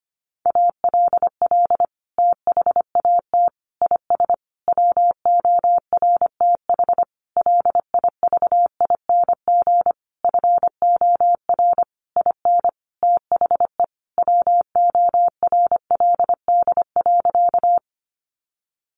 Morse code, heard as a keyed tone, about 25 wpm, 700 Hz. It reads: ALL THAT IS WORTH LIVING FOR IN THE WORLD.
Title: Sentences from Top 1100 Words 25wpm